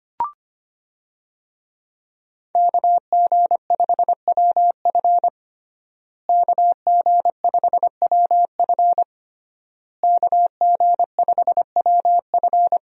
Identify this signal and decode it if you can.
Morse code, heard as a keyed tone, about 25 wpm, 700 Hz. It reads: KG5WF KG5WF KG5WF
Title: USA Call Signs - 5 Characters - Repeated 3x - 25wpm